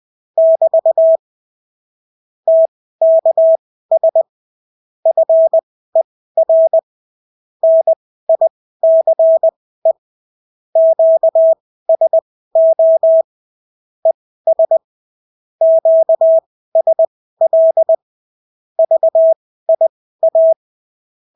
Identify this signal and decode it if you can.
Morse code, heard as a keyed tone, about 20 wpm, spaced as for 14 wpm, 650 Hz. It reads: <BT> TKS FER NICE QSO ES QSL VIA
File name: CW QSO libroCW4 20-10ppm